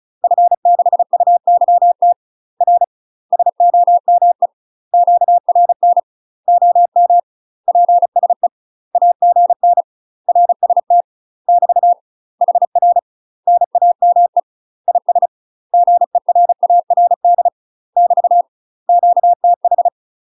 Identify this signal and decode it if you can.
Morse code, heard as a keyed tone, about 34 wpm, 700 Hz. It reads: F6UYT R SOME QRN OM PSE AGN RST <BT> HR NAME IS GERARD <BT> QTH